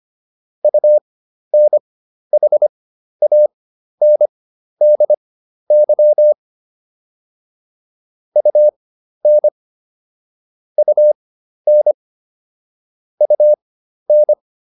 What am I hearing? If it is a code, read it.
Morse code, heard as a keyed tone, about 25 wpm, spaced as for 12 wpm, 600 Hz. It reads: UNHANDY UN UN UN